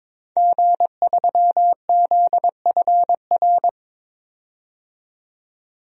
Morse code, heard as a keyed tone, about 22 wpm, 700 Hz. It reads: G3ZFR
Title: DXCC-100 Call Signs - 5 Characters - 22wpm